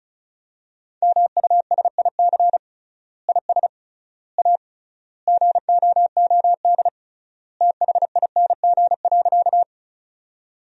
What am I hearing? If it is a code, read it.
Morse code, heard as a keyed tone, about 36 wpm, 700 Hz. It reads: MUSIC IS A GOOD THING.
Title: 3x Word-Spacing - Sentences from Top 300 Words - 35wpm